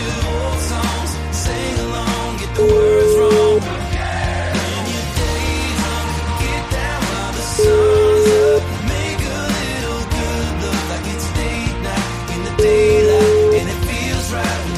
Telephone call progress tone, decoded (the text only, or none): ringback tone